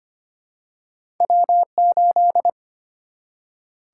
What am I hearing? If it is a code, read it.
Morse code, heard as a keyed tone, about 25 wpm, 700 Hz. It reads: W 8
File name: Two Letter-Number Combinations - 25wpm